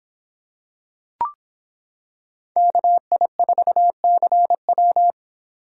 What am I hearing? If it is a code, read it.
Morse code, heard as a keyed tone, about 26 wpm, 700 Hz. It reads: KI4CW